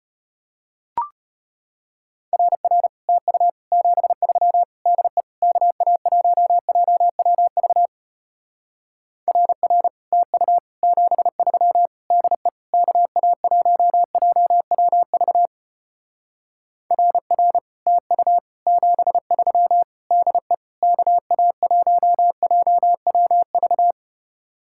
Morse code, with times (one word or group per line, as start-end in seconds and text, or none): 2.3-2.9 RR
3.1-3.5 TU
3.7-4.6 73
4.9-5.2 DE
5.4-7.9 KA1JWV
9.3-9.9 RR
10.1-10.6 TU
10.8-11.9 73
12.1-12.5 DE
12.7-15.5 KA1JWV
16.9-17.6 RR
17.9-18.4 TU
18.7-19.8 73
20.1-20.5 DE
20.8-23.9 KA1JWV